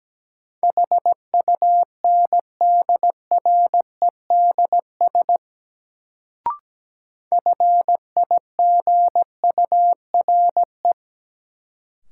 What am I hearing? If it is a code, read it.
Morse code, heard as a keyed tone, about 17 wpm, 700 Hz. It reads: HUNDREDS FIGURE
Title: Sets of 4 Words - New Word Drill - 17wpm